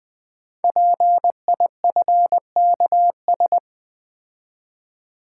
Morse code, heard as a keyed tone, about 20 wpm, 700 Hz. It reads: PIFKS
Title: Code Groups - 5 Letters - 20wpm